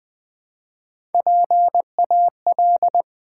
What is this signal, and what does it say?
Morse code, heard as a keyed tone, about 20 wpm, 700 Hz. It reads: PAL
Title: Build A Word - 1 Letter At A Time - 4-Letter Words - 20wpm